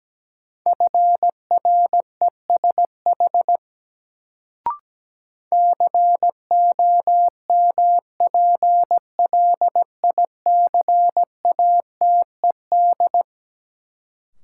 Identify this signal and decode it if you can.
Morse code, heard as a keyed tone, about 17 wpm, 700 Hz. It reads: FRESH COMPLICATED
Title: Sentences from Top 1200 Words - New Word Drill - 17wpm